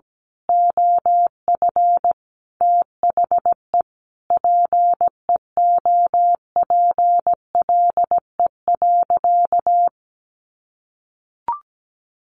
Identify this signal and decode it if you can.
Morse code, heard as a keyed tone, about 17 wpm, 700 Hz. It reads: OF THE PEOPLE.
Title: Sentences from Top 1500 Words 17wpm